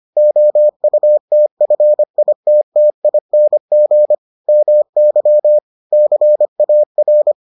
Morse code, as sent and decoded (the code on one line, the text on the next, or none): --- ..- - ..-. .. - - .. -. --. / -- -.-- / -.-. .- .-.
OUTFITTING MY CAR